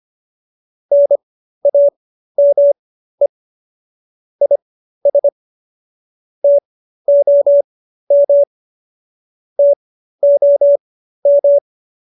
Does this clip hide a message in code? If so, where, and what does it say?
Yes: Morse code from 0.9 to 11.6 s, heard as a keyed tone, about 25 wpm, spaced as for 13 wpm, 570 Hz. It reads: NAME IS TOM TOM